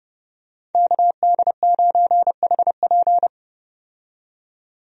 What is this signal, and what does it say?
Morse code, heard as a keyed tone, about 30 wpm, 700 Hz. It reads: KD9HP
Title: USA Call Signs - 5 Characters - 30wpm